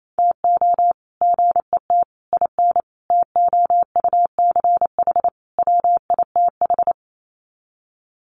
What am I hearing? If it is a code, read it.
Morse code, heard as a keyed tone, about 28 wpm, 700 Hz. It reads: TO GET IN TOUCH WITH
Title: Sets of 5 Words Encore2 28wpm